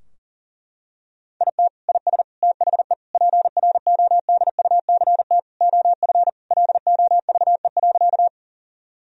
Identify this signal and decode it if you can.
Morse code, heard as a keyed tone, about 37 wpm, 700 Hz. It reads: IT IS THE PRODUCT OF LOVE.
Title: Sentences from Top 1000 Words - Review Part 2 - 40wpm